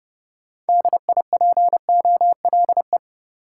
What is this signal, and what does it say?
Morse code, heard as a keyed tone, about 29 wpm, 700 Hz. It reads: DIPOLE